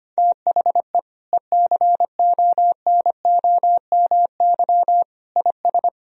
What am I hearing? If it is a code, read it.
Morse code, heard as a keyed tone, about 25 wpm, 700 Hz. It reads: THE ECONOMY IS
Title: Sentences from Top 1000 Words - Review Part 3 - 25wpm